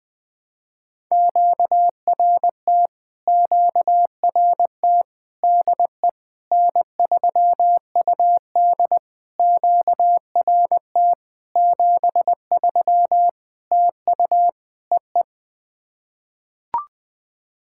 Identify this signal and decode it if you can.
Morse code, heard as a keyed tone, about 21 wpm, 700 Hz. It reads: QRT QRT DE N3UD QRT 73 TU EE
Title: POTA QSO Practice - 20wpm